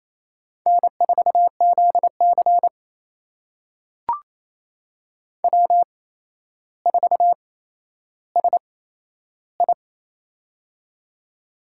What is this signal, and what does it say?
Morse code, heard as a keyed tone, about 29 wpm, spaced as for 12 wpm, 700 Hz. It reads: N4ZC W4SI
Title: USA Call Signs - 4 Characters - 8x Character Spacing - 28wpm